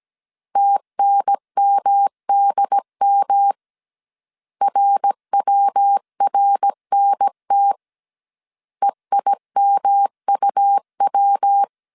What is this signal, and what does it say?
Morse code, heard as a keyed tone, about 17 wpm, 800 Hz. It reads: TNMDM RWRNT EIMUW